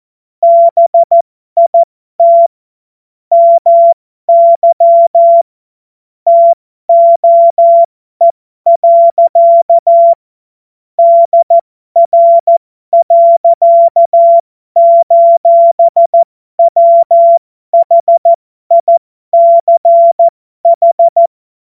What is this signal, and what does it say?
Morse code, heard as a keyed tone, about 14 wpm, spaced as for 12 wpm, 680 Hz. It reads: BIT MY TOE. DR.:WHICH
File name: Morsekurs / Witze